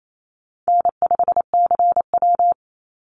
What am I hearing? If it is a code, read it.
Morse code, heard as a keyed tone, about 28 wpm, 700 Hz. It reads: N5CW
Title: Easy Call Signs - 28wpm